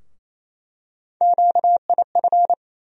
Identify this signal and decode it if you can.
Morse code, heard as a keyed tone, about 28 wpm, 700 Hz. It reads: QIF
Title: USA Call Sign Postfixes - 3 Letters 28wpm